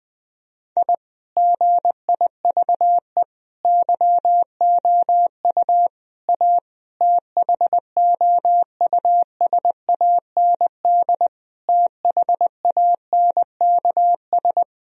Morse code, heard as a keyed tone, about 20 wpm, 700 Hz. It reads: I GIVE YOU A THOUSAND THANKS
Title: Sentences from Top 1000 Words - Review Part 4 - 20wpm